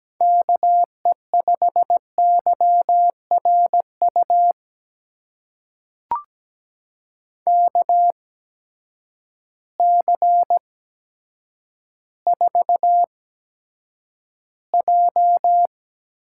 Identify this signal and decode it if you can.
Morse code, heard as a keyed tone, about 17 wpm, spaced as for 8 wpm, 700 Hz. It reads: KE5YRU KC4J
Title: USA Call Signs - 6 Characters - 8x Character Spacing - 17wpm